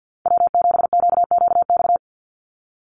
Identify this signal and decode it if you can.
Morse code, heard as a keyed tone, about 50 wpm, 700 Hz. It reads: W7QQX
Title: USA Call Signs 50wpm